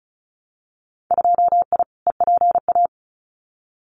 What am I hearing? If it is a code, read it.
Morse code, heard as a keyed tone, about 34 wpm, 700 Hz. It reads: 2I EPA
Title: Winter Field Day Exchanges 35wpm